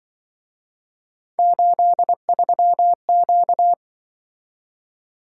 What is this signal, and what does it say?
Morse code, heard as a keyed tone, about 23 wpm, 700 Hz. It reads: 83Q
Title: Speed-Racing - Three Letter-Number Combinations - 20wpm